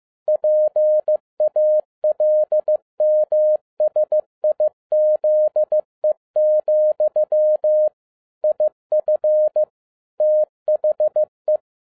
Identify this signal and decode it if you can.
Morse code, heard as a keyed tone, about 15 wpm, 610 Hz. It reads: PALMSIZE, IF THE
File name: 2011-07-mp3 / n22 QRP OTAKU 15wpm